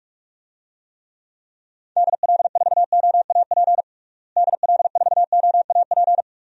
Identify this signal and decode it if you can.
Morse code, heard as a keyed tone, about 44 wpm, 700 Hz. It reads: DL4OAP DL4OAP
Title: USA Call Signs - 6 Characters - Repeated 3x - 45wpm